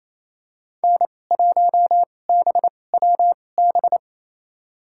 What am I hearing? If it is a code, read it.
Morse code, heard as a keyed tone, about 28 wpm, spaced as for 20 wpm, 700 Hz. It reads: N1BWB
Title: USA Call Signs - 5 Characters - 2x Character Spacing - 28wpm